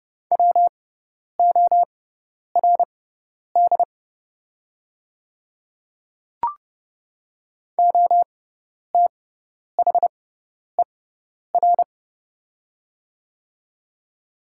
Morse code, heard as a keyed tone, about 30 wpm, spaced as for 10 wpm, 700 Hz. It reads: WORD OTHER
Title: 6x Character Spacing - Top 200 Words - 30wpm